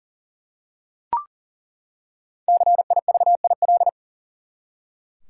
Morse code, heard as a keyed tone, about 39 wpm, 700 Hz. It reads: CIVIL
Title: Five Letter Words - 40wpm